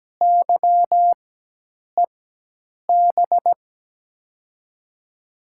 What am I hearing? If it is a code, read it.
Morse code, heard as a keyed tone, about 17 wpm, spaced as for 8 wpm, 700 Hz. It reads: YEB
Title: USA Call Signs - 6 Characters - 4x Character Spacing - 17wpm